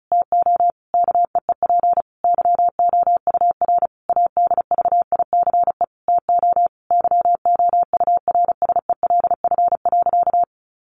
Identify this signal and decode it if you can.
Morse code, heard as a keyed tone, about 34 wpm, 700 Hz. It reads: TO KEEP YOUR ADVICE TO YOURSELF.
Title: Sentences from Top 1100 Words 35wpm